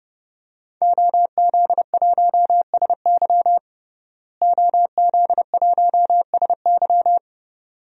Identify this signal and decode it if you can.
Morse code, heard as a keyed tone, about 29 wpm, 700 Hz. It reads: OZ1SY OZ1SY
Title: DXCC-100 Call Signs - 5 Characters - Repeated 2x - 30wpm